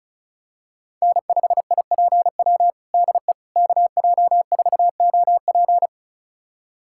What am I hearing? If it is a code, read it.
Morse code, heard as a keyed tone, about 36 wpm, 700 Hz. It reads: N5IPW DE KJ4OP
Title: USA Callsign DE Callsign - 5 Characters - 35wpm